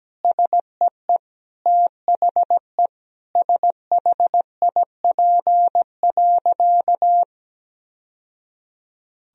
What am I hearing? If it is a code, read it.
Morse code, heard as a keyed tone, about 17 wpm, 700 Hz. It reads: SEE THE SHIP.
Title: Sentences from Top 1000 Words - Review Part 2 - 17wpm